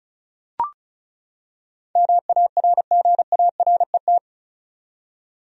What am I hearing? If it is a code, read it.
Morse code, heard as a keyed tone, about 36 wpm, 700 Hz. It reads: MARGARET
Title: USA Common Names - 35wpm